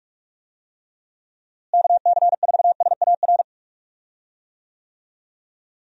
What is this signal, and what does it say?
Morse code, heard as a keyed tone, about 46 wpm, 700 Hz. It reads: KC4SAR
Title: USA Call Signs - 6 Characters - Repeated 2x - 45wpm